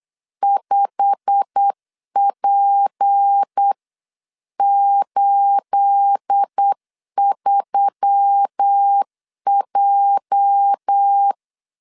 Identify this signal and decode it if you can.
Morse code, heard as a keyed tone, about 8 wpm, 800 Hz. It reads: 5P83J
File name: CW / DET / 1 9 1500C S A 8wpm 5637